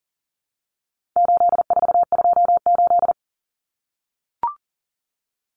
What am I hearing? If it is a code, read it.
Morse code, heard as a keyed tone, about 40 wpm, 700 Hz. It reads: 8428
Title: Numbers - 4-Digit 40wpm